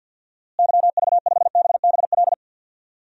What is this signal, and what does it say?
Morse code, heard as a keyed tone, about 52 wpm, 700 Hz. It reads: YV5BBL
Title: DXCC-100 Call Signs - 6 Characters - Repeated 3x - 50wpm